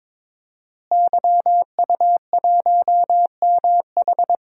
Morse code, heard as a keyed tone, about 21 wpm, 700 Hz. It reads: YU1MH
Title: DXCC-100 Call Signs - 5 Characters - Repeated 3x - 22wpm